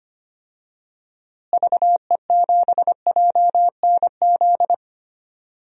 Morse code, heard as a keyed tone, about 25 wpm, 700 Hz. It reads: VE7JNZ
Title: DXCC-100 Call Signs - 6 Characters - Repeated 2x - 25wpm